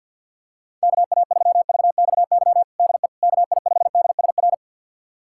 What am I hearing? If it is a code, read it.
Morse code, heard as a keyed tone, about 50 wpm, 700 Hz. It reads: KA3VXY DE KI5DSR
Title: USA Callsign DE Callsign 50wpm